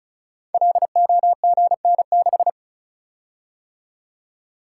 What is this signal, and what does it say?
Morse code, heard as a keyed tone, about 35 wpm, 690 Hz. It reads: LOGN6
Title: Code Groups - 5 Letter:Numbers - 35wpm